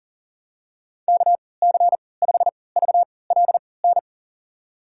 Morse code, heard as a keyed tone, about 37 wpm, spaced as for 22 wpm, 700 Hz. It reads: KC5VLN
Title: USA Call Signs - 6 Characters - 3x Character Spacing - 40wpm